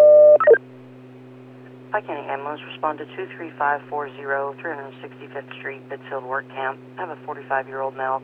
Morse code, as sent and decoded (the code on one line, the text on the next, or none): none